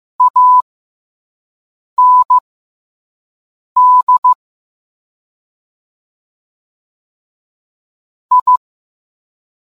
Morse code, heard as a keyed tone, about 15 wpm, spaced as for 5 wpm, 1000 Hz. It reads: AND I